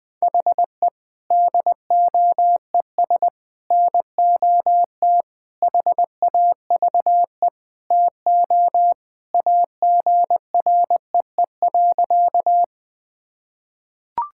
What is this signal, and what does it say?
Morse code, heard as a keyed tone, about 21 wpm, 700 Hz. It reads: HE DOES NOT HAVE TO AGREE.
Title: Sentences with 4-Word Phrases - 20wpm